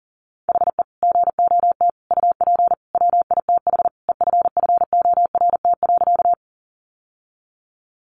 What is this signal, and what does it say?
Morse code, heard as a keyed tone, about 40 wpm, 700 Hz. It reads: HE GOT UP WITH EFFORT.
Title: Sentences from Top 1200 Words 40wpm